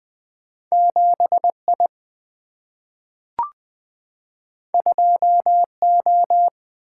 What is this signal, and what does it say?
Morse code, heard as a keyed tone, about 20 wpm, 700 Hz. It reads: 7I 2O
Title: Single-Letter Single-Number 20wpm